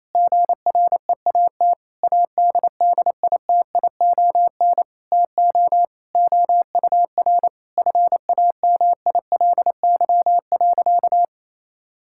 Morse code, heard as a keyed tone, about 28 wpm, 700 Hz. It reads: GREAT ADDITION TO OUR FAMILY.
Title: Sentences from Top 1000 Words - Review Part 6 - 28wpm